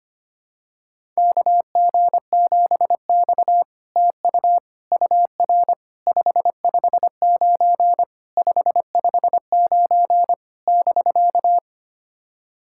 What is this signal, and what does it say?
Morse code, heard as a keyed tone, about 25 wpm, 700 Hz. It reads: KG7X TU UR 559 559 <BK>
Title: SOTA QSO Practice 25wpm